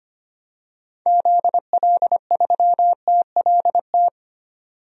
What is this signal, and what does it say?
Morse code, heard as a keyed tone, about 25 wpm, 700 Hz. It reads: ZL3TLT
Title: DXCC-100 Call Signs - 6 Characters - 25wpm